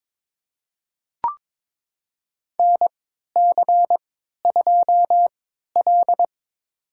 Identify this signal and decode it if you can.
Morse code, heard as a keyed tone, about 21 wpm, spaced as for 12 wpm, 700 Hz. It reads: NC2L
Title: USA Call Signs - 4 Characters - 3x Character Spacing - 22wpm